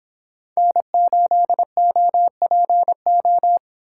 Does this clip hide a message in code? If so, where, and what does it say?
Yes: Morse code from 0.6 to 3.6 s, heard as a keyed tone, about 26 wpm, 700 Hz. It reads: N8OPO